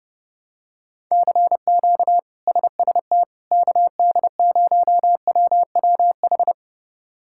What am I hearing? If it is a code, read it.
Morse code, heard as a keyed tone, about 30 wpm, 700 Hz. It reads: CQ SST KD0WWH
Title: K1USN SST QSO Practice - 30wpm